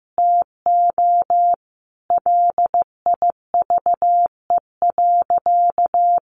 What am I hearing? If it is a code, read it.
Morse code, heard as a keyed tone, about 15 wpm, 700 Hz. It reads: TO LIVE.
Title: Sentences from Top 300 Words 15wpm